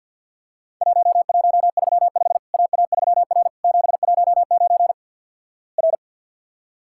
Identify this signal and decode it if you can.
Morse code, heard as a keyed tone, about 50 wpm, 700 Hz. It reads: 1135 AA3R 719 R
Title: Worked All Europe DX Contest - QTC Exchange with Repeats - 50wpm